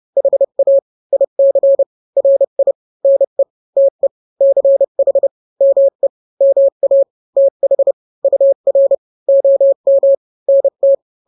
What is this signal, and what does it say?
Morse code, heard as a keyed tone, about 30 wpm, 550 Hz. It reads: HA IC RI NE TE CH ME MA TH UR OM NT
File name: n-grams / 2-grams 30 25 WPM